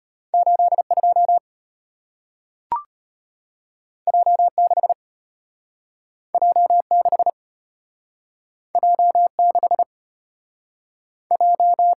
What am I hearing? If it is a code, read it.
Morse code, heard as a keyed tone, about 35 wpm, 700 Hz. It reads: 82 J6 J6 J6 J